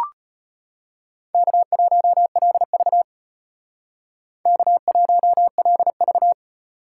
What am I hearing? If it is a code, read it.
Morse code, heard as a keyed tone, about 32 wpm, 700 Hz. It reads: K1LV K1LV